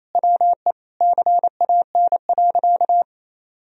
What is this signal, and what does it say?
Morse code, heard as a keyed tone, about 27 wpm, 700 Hz. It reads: WE CAN.
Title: Sentences from Top 1000 Words - Review Part 2 - 28wpm